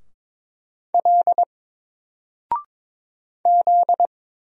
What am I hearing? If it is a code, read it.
Morse code, heard as a keyed tone, about 22 wpm, 700 Hz. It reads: L Z
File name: Single-Letter-Number-22wpm